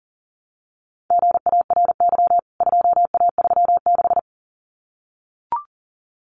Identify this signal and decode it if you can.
Morse code, heard as a keyed tone, about 40 wpm, 700 Hz. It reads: GARY 2A36
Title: CWT Exchange 40wpm